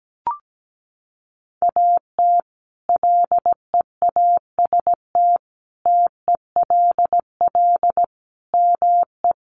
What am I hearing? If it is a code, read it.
Morse code, heard as a keyed tone, about 17 wpm, 700 Hz. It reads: AT LEAST TELL ME
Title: Sentences from Top 700 Words 17wpm